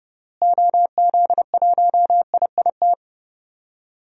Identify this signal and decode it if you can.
Morse code, heard as a keyed tone, about 30 wpm, 700 Hz. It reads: OZ1IIT
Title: DXCC-100 Call Signs - 6 Characters - Repeated 3x - 30wpm